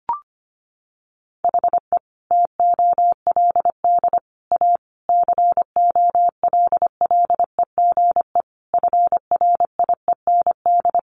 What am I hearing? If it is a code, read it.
Morse code, heard as a keyed tone, about 25 wpm, 700 Hz. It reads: HE TOLD A COLLEGE FRIEND
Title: Sentences from Top 1300 Words 25wpm